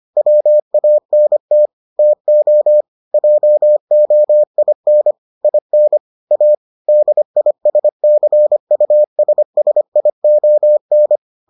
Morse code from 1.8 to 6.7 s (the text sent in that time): TO JOIN IN A